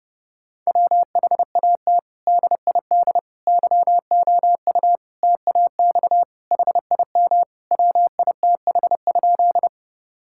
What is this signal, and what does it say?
Morse code, heard as a keyed tone, about 30 wpm, 700 Hz. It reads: WHAT DID YOU TAX HIM WITH?
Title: Sentences from Top 1000 Words - Review Part 5 - 30wpm